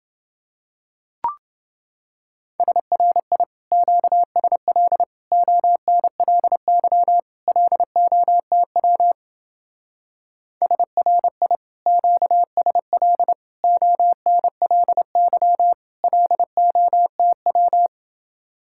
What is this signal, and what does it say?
Morse code, heard as a keyed tone, about 28 wpm, 700 Hz. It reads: SRI QSL ONLY LOTW SRI QSL ONLY LOTW